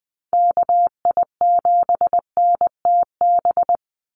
Morse code, heard as a keyed tone, about 20 wpm, 700 Hz. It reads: KI7NTB
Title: USA Call Signs 20wpm